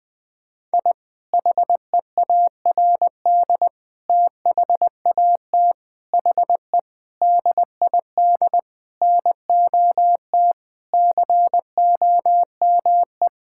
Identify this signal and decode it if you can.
Morse code, heard as a keyed tone, about 20 wpm, 700 Hz. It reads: I HEARD THAT HE DID NOT COME
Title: Sentences with 4-Word Phrases - 20wpm